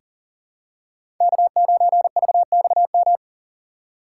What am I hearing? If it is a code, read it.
Morse code, heard as a keyed tone, about 39 wpm, 700 Hz. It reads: K9VXM
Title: USA Call Signs - 5 Characters - 3x Character Spacing - 40wpm